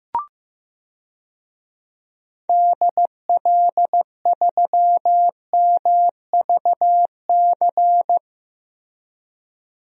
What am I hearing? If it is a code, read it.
Morse code, heard as a keyed tone, about 16 wpm, 700 Hz. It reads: DL3MVC
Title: DXCC-100 Call Signs - 6 Characters - Repeated 3x - 15wpm